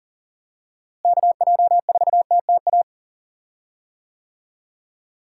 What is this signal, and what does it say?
Morse code, heard as a keyed tone, about 40 wpm, 700 Hz. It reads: KJ4TTA
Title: USA Call Signs - 6 Characters - Repeated 3x - 40wpm